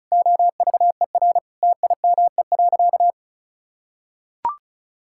Morse code, heard as a keyed tone, about 36 wpm, 700 Hz. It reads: OVER TIME.
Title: Sentences from Top 1000 Words - Review Part 6 - 35wpm